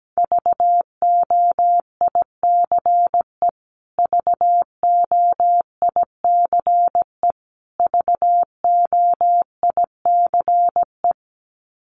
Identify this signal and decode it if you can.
Morse code, heard as a keyed tone, about 17 wpm, 700 Hz. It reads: VOICE VOICE VOICE
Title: Top 500 Words Repeated 17wpm